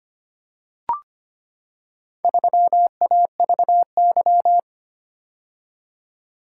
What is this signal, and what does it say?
Morse code, heard as a keyed tone, about 25 wpm, 700 Hz. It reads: 3AVY
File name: practice / 25wpm / Four Letter-Number Combinations - 25wpm